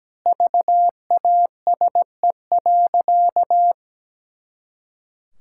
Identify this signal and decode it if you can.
Morse code, heard as a keyed tone, about 17 wpm, 700 Hz. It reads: VASE.
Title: Pangram Challenge v2 - 17wpm